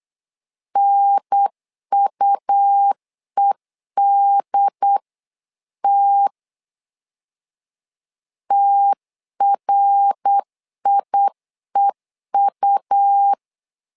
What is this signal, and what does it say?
Morse code, heard as a keyed tone, about 8 wpm, 800 Hz. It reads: NUEDT TRIEU
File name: LEZIONI / 3 MWRO / 3 MWR 8wpm 2444